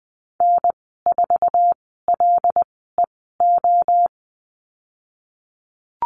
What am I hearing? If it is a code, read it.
Morse code, heard as a keyed tone, about 20 wpm, spaced as for 15 wpm, 700 Hz. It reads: N4LEO
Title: 088 Lesson 15g - Number 5 - Call Signs 20wpm